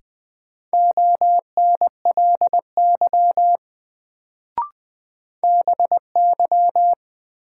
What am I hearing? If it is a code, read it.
Morse code, heard as a keyed tone, about 20 wpm, 700 Hz. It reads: ONLY BY
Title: Sets of 2 Words - New Word Drill - 20wpm